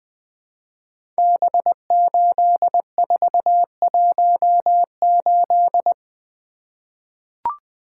Morse code, heard as a keyed tone, about 20 wpm, 700 Hz. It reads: B8418
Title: Code Groups - 5 Letter:Numbers - 20wpm